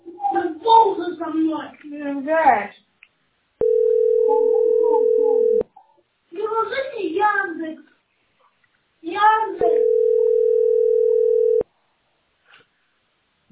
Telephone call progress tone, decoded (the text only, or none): ringback tone